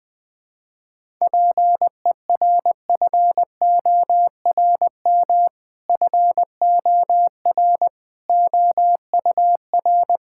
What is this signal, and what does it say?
Morse code, heard as a keyed tone, about 21 wpm, 700 Hz. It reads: PERFORM FOR OUR
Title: Sentences from Top 1000 Words - Review Part 5 - 20wpm